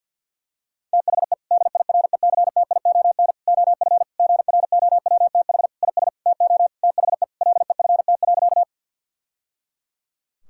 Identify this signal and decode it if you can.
Morse code, heard as a keyed tone, about 52 wpm, 690 Hz. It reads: THE DIRECTION OF GROWTH IS TO THE LEFT.
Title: Sentences from Top 1000 Words - Review Part 5 - 50wpm